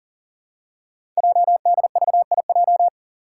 Morse code, heard as a keyed tone, about 39 wpm, 690 Hz. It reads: JDVIJ
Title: Code Groups - 5 Letter:Numbers - 40wpm